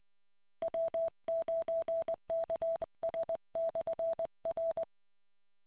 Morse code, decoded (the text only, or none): W9KS/R